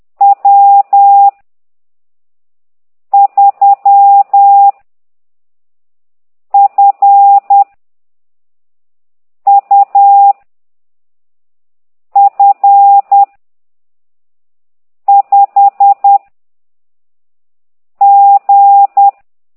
Morse code, heard as a keyed tone, about 10 wpm, 800 Hz. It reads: W 3 F U F 5 G